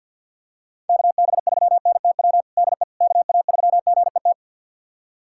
Morse code, heard as a keyed tone, about 51 wpm, 700 Hz. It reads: KB3NTW DE KA3GET
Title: USA Callsign DE Callsign - 6 Characters - 50wpm